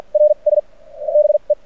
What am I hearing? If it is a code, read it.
Morse code, heard as a keyed tone, about 49 wpm, spaced as for 37 wpm, 600 Hz. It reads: R S BE
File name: archived / CRSnet